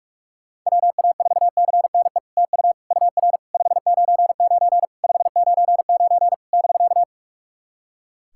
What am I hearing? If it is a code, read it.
Morse code, heard as a keyed tone, about 44 wpm, 660 Hz. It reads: WA4CNE TU UR 599 599 <BK>
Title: SOTA QSO Practice 45wpm